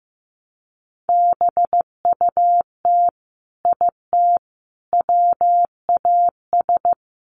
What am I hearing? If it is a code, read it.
Morse code, heard as a keyed tone, about 15 wpm, 700 Hz. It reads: BUT IT WAS
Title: Sentences from Top 1000 Words 15wpm